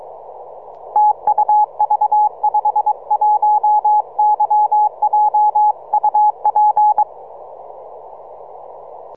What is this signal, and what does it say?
Morse code, heard as a keyed tone, about 22 wpm, 840 Hz, with radio noise. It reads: TU V51YJ UP
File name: simple-file-list / V51YJ 6MCW